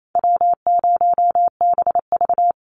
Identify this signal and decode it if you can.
Morse code, heard as a keyed tone, about 28 wpm, 700 Hz. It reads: W0BV
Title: Easy Call Signs Repeated 3x - 28wpm